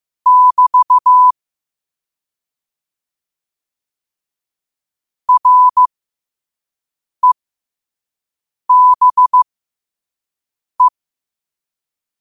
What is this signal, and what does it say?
Morse code, heard as a keyed tone, about 15 wpm, spaced as for 5 wpm, 1000 Hz. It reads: <BT> REBE
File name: qotdfiles / 5wpm